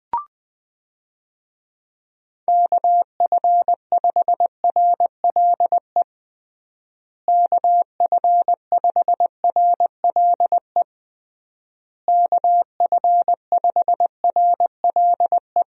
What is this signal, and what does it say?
Morse code, heard as a keyed tone, about 20 wpm, 700 Hz. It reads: KF5RLE KF5RLE KF5RLE
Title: USA Call Signs - 6 Characters - Repeated 3x - 20wpm